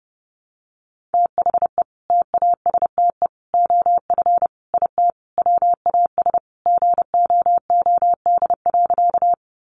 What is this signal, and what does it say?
Morse code, heard as a keyed tone, about 30 wpm, 700 Hz. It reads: THE TASTE OF IT WAS GOOD.
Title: Sentences from Top 1100 Words 30wpm